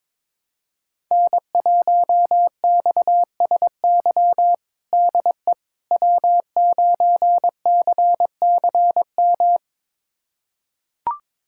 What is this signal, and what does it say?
Morse code, heard as a keyed tone, about 21 wpm, 700 Hz. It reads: N1XSY DE W9CCM
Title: USA Callsign DE Callsign - 5 Characters - 22wpm